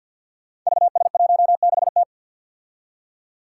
Morse code, heard as a keyed tone, about 47 wpm, 700 Hz. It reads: VS16T
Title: Five Challenging Letter-Number Combinations 50wpm